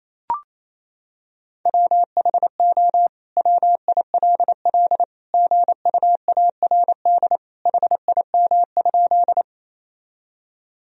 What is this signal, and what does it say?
Morse code, heard as a keyed tone, about 28 wpm, 700 Hz. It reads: WHO WILL GUARD HIM?
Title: Sentences from Top 1000 Words - Review Part 4 - 28wpm